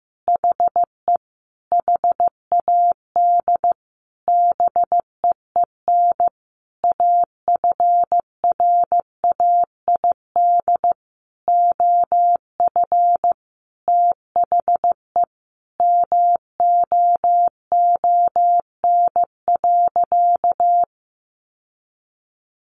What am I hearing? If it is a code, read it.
Morse code, heard as a keyed tone, about 15 wpm, 700 Hz. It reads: HE HAD BEEN AFRAID OF THE MOON.
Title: Sentences from Top 1500 Words 15wpm